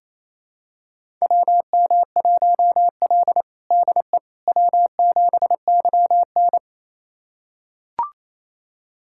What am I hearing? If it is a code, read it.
Morse code, heard as a keyed tone, about 28 wpm, 690 Hz. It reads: WM1L DE W7YN
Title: USA Callsign DE Callsign - 4 Characters - 28wpm